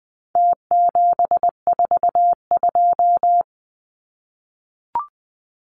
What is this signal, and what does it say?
Morse code, heard as a keyed tone, about 20 wpm, 700 Hz. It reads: T742
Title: Numbers - Contest Style - 4-Digit 20wpm